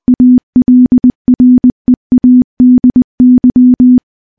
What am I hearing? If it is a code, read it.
Morse code, heard as a keyed tone, about 19 wpm, 250 Hz. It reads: ALREADY
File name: morse / morse-training-20WPM